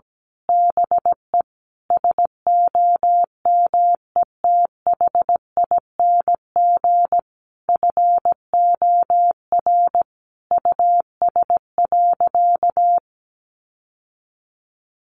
Morse code, heard as a keyed tone, about 17 wpm, 700 Hz. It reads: BE SOMETHING FOR US.
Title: Sentences from Top 1200 Words 17wpm